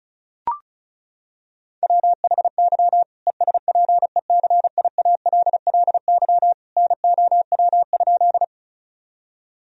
Morse code, heard as a keyed tone, about 36 wpm, 700 Hz. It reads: WHY ESPECIALLY NOW?
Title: Sentences from Top 1000 Words - Review Part 4 - 35wpm